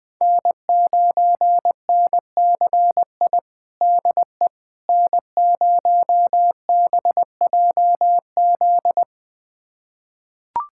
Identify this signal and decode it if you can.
Morse code, heard as a keyed tone, about 20 wpm, 700 Hz. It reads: N9NCI DE N0BJZ
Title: USA Callsign DE Callsign - 5 Characters - 20wpm